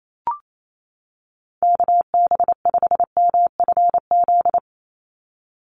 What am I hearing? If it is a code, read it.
Morse code, heard as a keyed tone, about 28 wpm, 700 Hz. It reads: KB5MFZ
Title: USA Call Signs 28wpm